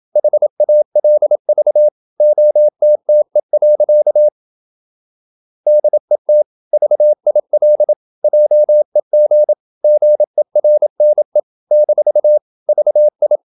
Morse code, heard as a keyed tone, about 27 wpm, 600 Hz. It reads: HALV OTTE. DET VIL JEG GERNE - VI
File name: Ferie QRQ 27wpm